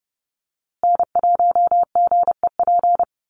Morse code, heard as a keyed tone, about 30 wpm, 700 Hz. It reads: N1GEP
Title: USA Call Signs Repeated 3x 30wpm